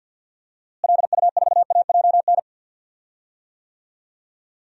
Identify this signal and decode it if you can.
Morse code, heard as a keyed tone, about 47 wpm, 700 Hz. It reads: LU4AJN
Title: DXCC-100 Call Signs - 6 Characters - Repeated 2x - 50wpm